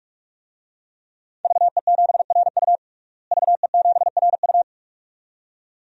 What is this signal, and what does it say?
Morse code, heard as a keyed tone, about 46 wpm, 700 Hz. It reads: VE7RU VE7RU
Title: DXCC-100 Call Signs - 5 Characters - Repeated 2x - 45wpm